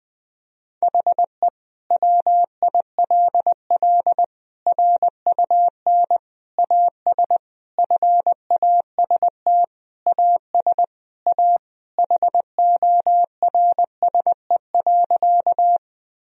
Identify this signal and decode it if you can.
Morse code, heard as a keyed tone, about 20 wpm, 700 Hz. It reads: HE WILL RUN AS FAST AS A HORSE.
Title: Sentences from Top 1000 Words - Review Part 2 - 20wpm